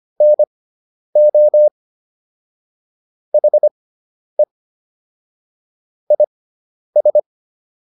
Morse code, heard as a keyed tone, about 25 wpm, spaced as for 10 wpm, 600 Hz. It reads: NO HE IS